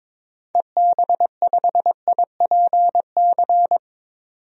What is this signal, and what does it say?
Morse code, heard as a keyed tone, about 21 wpm, 700 Hz. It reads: EB5IPC
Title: DXCC-100 Call Signs - 6 Characters - 22wpm